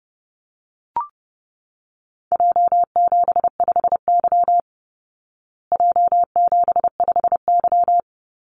Morse code, heard as a keyed tone, about 30 wpm, 700 Hz. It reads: J75Y J75Y
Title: Call Signs Repeated 3x - 30wpm